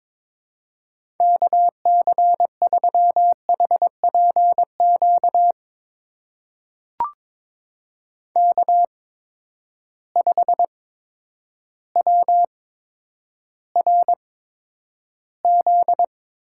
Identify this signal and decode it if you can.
Morse code, heard as a keyed tone, about 22 wpm, spaced as for 10 wpm, 700 Hz. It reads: KC3HPQ K5WRZ
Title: USA Call Signs - 8x Character Spacing - 22wpm